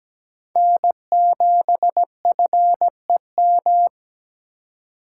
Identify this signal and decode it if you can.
Morse code, heard as a keyed tone, about 17 wpm, 700 Hz. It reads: N7FEM
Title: USA Call Signs - 5 Characters - 2x Character Spacing - 17wpm